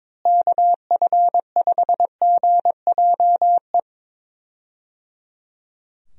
Morse code, heard as a keyed tone, about 21 wpm, 700 Hz. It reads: KF5GJE
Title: USA Call Signs - 6 Characters - Repeated 3x - 22wpm